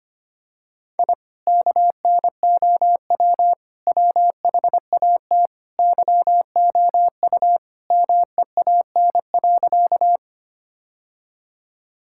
Morse code, heard as a keyed tone, about 25 wpm, 700 Hz. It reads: I KNOW WHAT YOU MEAN.
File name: practice / 25wpm / Sentences with 2-Word Phrases - 25wpm